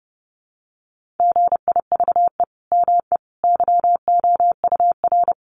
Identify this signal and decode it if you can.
Morse code, heard as a keyed tone, about 30 wpm, 700 Hz. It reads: GIVE ME YOUR
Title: Sentences from Top 200 Words 30wpm